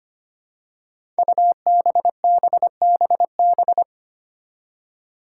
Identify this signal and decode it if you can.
Morse code, heard as a keyed tone, about 25 wpm, 700 Hz. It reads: UBBBB
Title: Five Challenging Letter-Number Combinations 25wpm